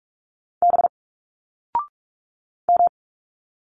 Morse code, heard as a keyed tone, about 46 wpm, 700 Hz. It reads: B M
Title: Single Letters 45wpm